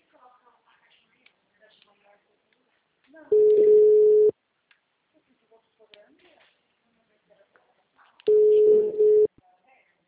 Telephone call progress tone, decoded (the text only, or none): ringback tone